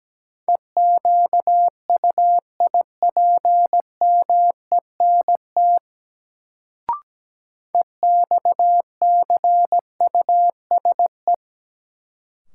Morse code, heard as a keyed tone, about 17 wpm, 700 Hz. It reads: EQUIPMENT EXCUSE
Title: Sentences from Top 700 Words - New Word Drill - 17wpm